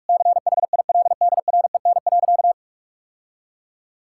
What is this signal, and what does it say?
Morse code, heard as a keyed tone, about 46 wpm, 700 Hz. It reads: CHILDREN.
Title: Sentences from Top 1000 Words - Review Part 6 - 45wpm